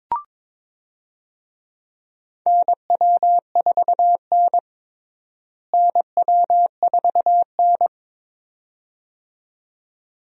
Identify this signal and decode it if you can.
Morse code, heard as a keyed tone, about 21 wpm, 700 Hz. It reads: NW4N NW4N
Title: USA Call Signs - 4 Characters - Repeated 2x - 22wpm